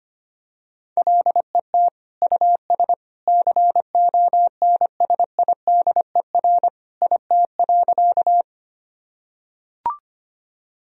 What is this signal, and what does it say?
Morse code, heard as a keyed tone, about 25 wpm, 700 Hz. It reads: LET US CONSIDER IT.
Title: Sentences from Top 1000 Words - Review Part 4 - 25wpm